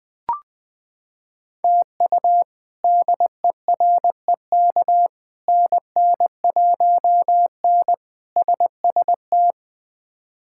Morse code, heard as a keyed tone, about 20 wpm, 700 Hz. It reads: TU DEREK NN1N SST
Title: K1USN SST QSO Practice - 20wpm